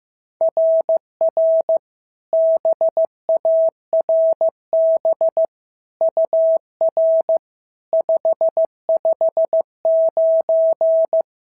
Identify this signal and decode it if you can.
Morse code, heard as a keyed tone, about 14 wpm, 650 Hz. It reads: RR BARB UR 559
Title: SOTA QSO Practice 15wpm